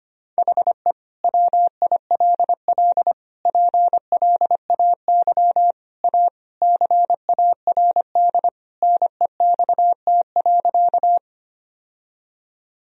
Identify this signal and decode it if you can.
Morse code, heard as a keyed tone, about 25 wpm, 700 Hz. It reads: HE WILL PLAY A CARD NEXT.
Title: Sentences from Top 1000 Words - Review Part 4 - 25wpm